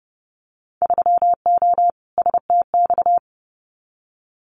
Morse code, heard as a keyed tone, about 30 wpm, 700 Hz. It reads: 3O STX
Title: Winter Field Day Exchanges 30wpm